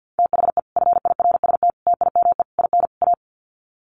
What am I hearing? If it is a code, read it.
Morse code, heard as a keyed tone, about 49 wpm, 700 Hz. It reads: THE FIRST TIME IN A
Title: Sets of 5 Words Encore 50wpm